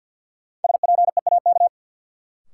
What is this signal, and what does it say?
Morse code, heard as a keyed tone, about 53 wpm, 700 Hz. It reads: SPEAK
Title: Sentences from Top 1100 Words - New Word Drill Alt - 50wpm